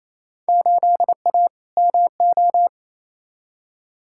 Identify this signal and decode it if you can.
Morse code, heard as a keyed tone, about 28 wpm, 700 Hz. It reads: 8A MO
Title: Field Day Exchanges Encore - 28wpm